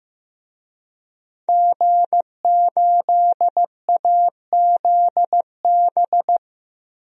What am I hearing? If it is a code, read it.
Morse code, heard as a keyed tone, about 15 wpm, 700 Hz. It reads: G8AZB
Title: DXCC-100 Call Signs - 5 Characters - 15wpm